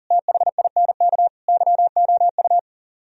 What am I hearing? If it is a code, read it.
Morse code, heard as a keyed tone, about 49 wpm, 700 Hz. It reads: THINK YOU
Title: Speed-Racing - Sets of 4 Words 40wpm